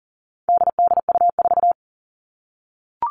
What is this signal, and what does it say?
Morse code, heard as a keyed tone, about 39 wpm, 700 Hz. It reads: DDU4
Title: All Cut Numbers - 4-Digit 40wpm